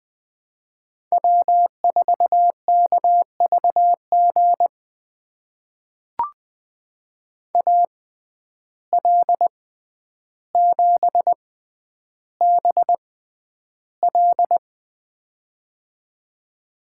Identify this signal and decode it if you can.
Morse code, heard as a keyed tone, about 20 wpm, spaced as for 10 wpm, 700 Hz. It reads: W4KVG AL7BL
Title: USA Call Signs - 5 Characters - 6x Character Spacing - 20wpm